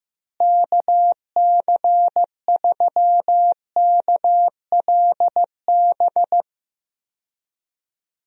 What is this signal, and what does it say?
Morse code, heard as a keyed tone, about 15 wpm, 700 Hz. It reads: KC3KLB
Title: USA Call Signs - 6 Characters - Repeated 3x - 15wpm